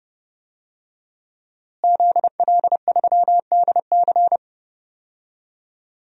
Morse code, heard as a keyed tone, about 32 wpm, 700 Hz. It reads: ZL3DC
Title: DXCC-100 Call Signs - 5 Characters - Repeated 3x - 30wpm